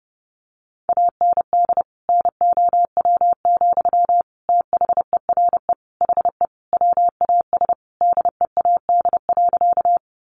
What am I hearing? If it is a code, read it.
Morse code, heard as a keyed tone, about 28 wpm, 700 Hz. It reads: AND NOW, THERE HE WAS DEAD.
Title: Sentences from Top 1500 Words 30wpm